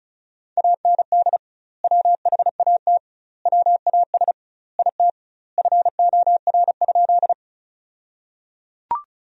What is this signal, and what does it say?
Morse code, heard as a keyed tone, about 34 wpm, 700 Hz. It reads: AND WHAT WAS IT FOR?
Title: 2x Word-Spacing - Sentences from Top 100 Words - 35wpm